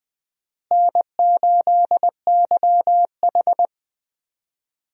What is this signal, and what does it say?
Morse code, heard as a keyed tone, about 20 wpm, 700 Hz. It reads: N8YH
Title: USA Call Signs - 4 Characters - Repeated 2x - 20wpm